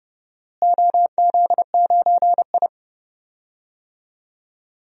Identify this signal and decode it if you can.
Morse code, heard as a keyed tone, about 32 wpm, 700 Hz. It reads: OZ9I
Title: DXCC-100 Call Signs - 4 Characters - Repeated 2x - 30wpm